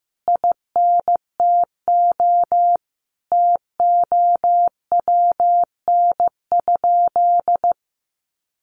Morse code, heard as a keyed tone, about 15 wpm, 700 Hz. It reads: INTO TOWN?
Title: Sentences from Top 1400 Words 15wpm